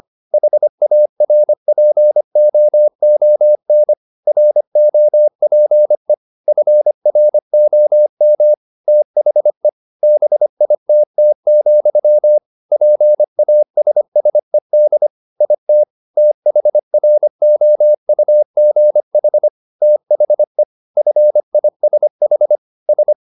Morse code, heard as a keyed tone, about 25 wpm, 600 Hz. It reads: HARPOON ROPE FROM THE BITT, PASSED IT THROUGH THE FISH S